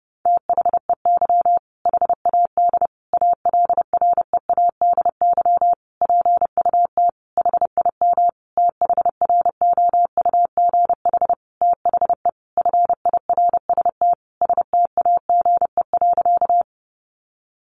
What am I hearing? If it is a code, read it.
Morse code, heard as a keyed tone, about 30 wpm, 700 Hz. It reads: THEY HAD ALREADY PUT HIM THROUGH THE FIRST STAGE.
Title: Sentences from Top 1200 Words 30wpm